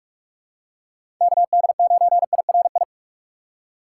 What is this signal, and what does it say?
Morse code, heard as a keyed tone, about 46 wpm, 690 Hz. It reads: KD9IRI
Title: USA Call Signs - 6 Characters - 3x Character Spacing - 45wpm